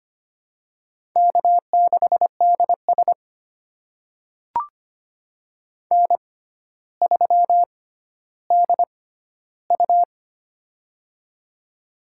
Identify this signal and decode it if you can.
Morse code, heard as a keyed tone, about 25 wpm, spaced as for 13 wpm, 700 Hz. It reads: K6DS N3DU
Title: USA Call Signs - 6x Character Spacing - 25wpm